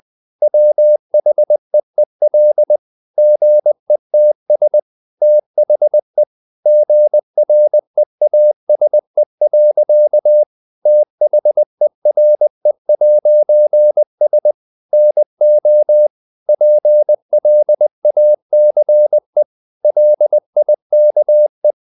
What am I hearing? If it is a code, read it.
Morse code, heard as a keyed tone, about 20 wpm, 600 Hz. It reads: WHEEL GETS THE GREASE. THERE'S NO PLACE LIKE
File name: CW / Practice / EnglishSayings20